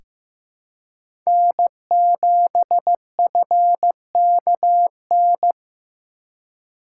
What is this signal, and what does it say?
Morse code, heard as a keyed tone, about 15 wpm, 700 Hz. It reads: N7FKN
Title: USA Call Signs - 5 Characters - 3x Character Spacing - 15wpm